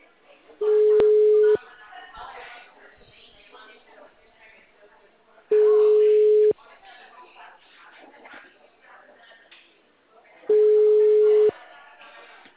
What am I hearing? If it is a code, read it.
Ringback tone (Europe, 425 Hz), with speech.